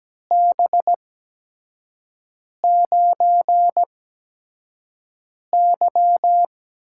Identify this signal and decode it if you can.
Morse code, heard as a keyed tone, about 17 wpm, spaced as for 5 wpm, 700 Hz. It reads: B9Y
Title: USA Call Signs - 6 Characters - 8x Character Spacing - 17wpm